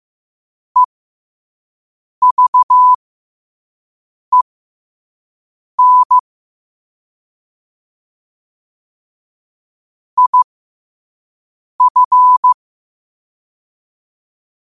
Morse code, heard as a keyed tone, about 15 wpm, spaced as for 5 wpm, 1000 Hz. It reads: EVEN IF
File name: qotdfiles / 5wpm